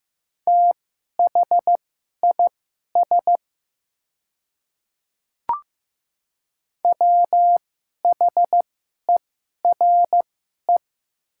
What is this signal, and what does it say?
Morse code, heard as a keyed tone, about 15 wpm, spaced as for 11 wpm, 700 Hz. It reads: THIS WHERE
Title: 2x Character Spacing - Top 300 Words - 15wpm